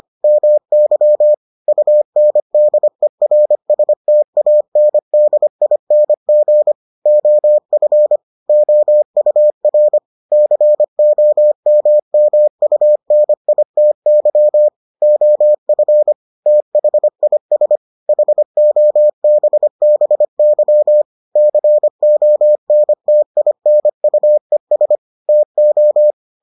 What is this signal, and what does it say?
Morse code, heard as a keyed tone, about 25 wpm, 600 Hz. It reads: MY UNDERSTANDING OF OUR COMMUNITY OF THIS HOBBY CONTINUES TO